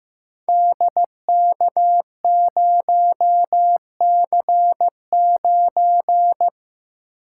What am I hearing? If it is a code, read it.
Morse code, heard as a keyed tone, about 15 wpm, 700 Hz. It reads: DK0C9
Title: Code Groups - 5 Letter:Numbers - 15wpm